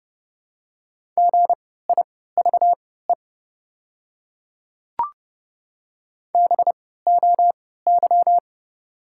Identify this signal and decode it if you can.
Morse code, heard as a keyed tone, about 32 wpm, spaced as for 17 wpm, 700 Hz. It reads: GIVE BOY
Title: 3x Character Spacing - Top 200 Words - 30wpm